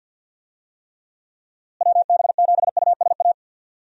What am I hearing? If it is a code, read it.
Morse code, heard as a keyed tone, about 51 wpm, 700 Hz. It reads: WB7USA
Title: USA Call Signs - 6 Characters - Repeated 3x - 50wpm